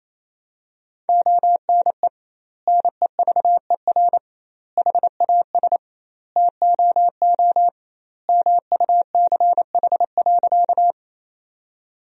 Morse code, heard as a keyed tone, about 28 wpm, 700 Hz. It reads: ONE NEVER HAS TOO MUCH.
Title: 2x Word-Spacing - Sentences from Top 300 Words - 28wpm